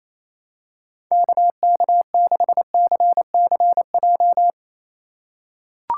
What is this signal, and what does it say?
Morse code, heard as a keyed tone, about 28 wpm, 700 Hz. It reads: KK6CCJ
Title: USA Call Signs - 6 Characters - 3x Character Spacing - 28wpm